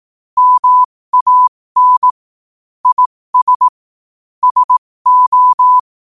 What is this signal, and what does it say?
Morse code, heard as a keyed tone, about 18 wpm, spaced as for 15 wpm, 1000 Hz. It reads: MAN IS SO
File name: qotdfiles / 15wpm